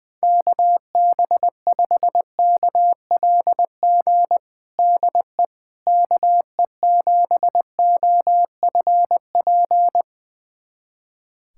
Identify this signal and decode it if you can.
Morse code, heard as a keyed tone, about 20 wpm, 700 Hz. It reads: KB5KLG DE KE7OFP
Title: USA Callsign DE Callsign 20wpm